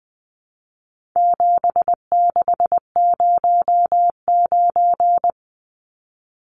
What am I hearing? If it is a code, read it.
Morse code, heard as a keyed tone, about 20 wpm, 700 Hz. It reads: 7609
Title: Numbers - 4-Digit 20wpm